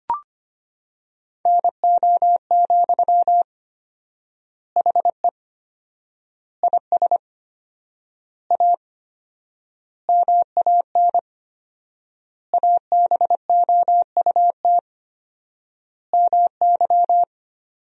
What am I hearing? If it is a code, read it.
Morse code, heard as a keyed tone, about 25 wpm, 700 Hz. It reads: NO, HE IS A MAN ABOUT MY
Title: 4x Word-Spacing - Sentences from Top 400 Words - 25wpm